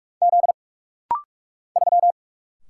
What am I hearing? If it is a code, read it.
Morse code, heard as a keyed tone, about 46 wpm, 700 Hz. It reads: Z 3
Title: Single Letter-Number Alt - Rapid-Fire 45wpm